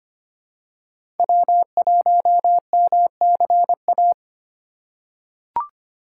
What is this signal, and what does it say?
Morse code, heard as a keyed tone, about 25 wpm, 700 Hz. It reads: W1MCA
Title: USA Call Signs - 5 Characters - 2x Character Spacing - 25wpm